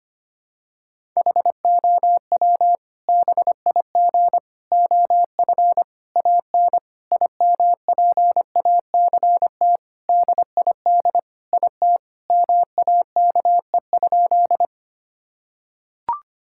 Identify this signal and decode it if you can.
Morse code, heard as a keyed tone, about 25 wpm, 700 Hz. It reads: HOW BIG OF AN IMPACT DID IT MAKE?
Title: Sentences from Top 1000 Words - Review Part 6 - 25wpm